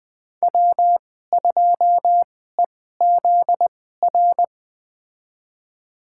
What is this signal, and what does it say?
Morse code, heard as a keyed tone, about 20 wpm, spaced as for 14 wpm, 700 Hz. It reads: W2EZR
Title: USA Call Signs - 5 Characters - 2x Character Spacing - 20wpm